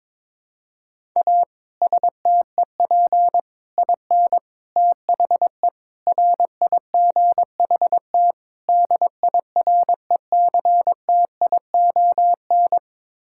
Morse code, heard as a keyed tone, about 22 wpm, 700 Hz. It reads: A STEP IN THE RIGHT DIRECTION
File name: practice / 22wpm / Sets of 6 Words - 22wpm